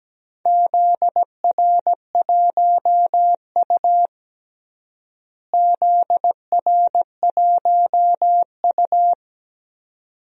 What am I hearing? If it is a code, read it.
Morse code, heard as a keyed tone, about 17 wpm, 700 Hz. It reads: ZR1U ZR1U
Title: DXCC-100 Call Signs - 4 Characters - Repeated 2x - 17wpm